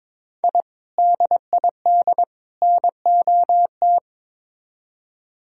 Morse code, heard as a keyed tone, about 22 wpm, 700 Hz. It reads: I DID NOT
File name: practice / 22wpm / Build A Phrase - 5-Word Phrases - 22wpm